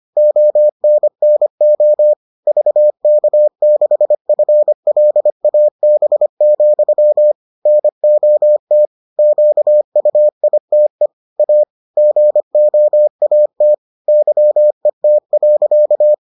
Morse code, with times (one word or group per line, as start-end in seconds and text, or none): 0.2-2.1 ONNO
2.5-7.3 VK6FLAB,
7.7-8.8 NOT
9.2-11.1 QUITE
11.4-11.6 A
12.0-13.7 GOAT
14.1-16.1 YET.